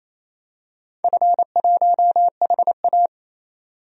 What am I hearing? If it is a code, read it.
Morse code, heard as a keyed tone, about 28 wpm, 700 Hz. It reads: F1HA
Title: DXCC-100 Call Signs - 4 Characters - 28wpm